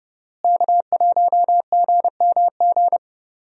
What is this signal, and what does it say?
Morse code, heard as a keyed tone, about 30 wpm, 700 Hz. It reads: K1GMG
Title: USA Call Signs - 5 Characters - Repeated 2x - 30wpm